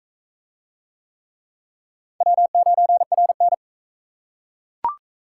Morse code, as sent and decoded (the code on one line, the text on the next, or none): .-- ----. .-. -.
W9RN